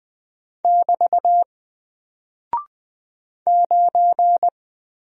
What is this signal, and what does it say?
Morse code, heard as a keyed tone, about 20 wpm, 700 Hz. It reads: <BT> 9
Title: Single Character (Letter:Number:Punctuation:Prosign) 20wpm